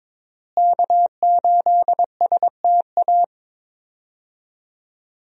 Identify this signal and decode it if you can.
Morse code, heard as a keyed tone, about 22 wpm, 700 Hz. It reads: K8STA
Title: USA Call Signs - 5 Characters - Repeated 2x - 22wpm